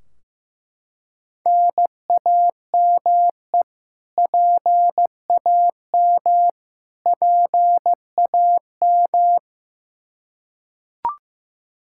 Morse code, heard as a keyed tone, about 15 wpm, 700 Hz. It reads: NAME PAM PAM
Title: Practice QSO - Exchange 1 - Beginning - 15wpm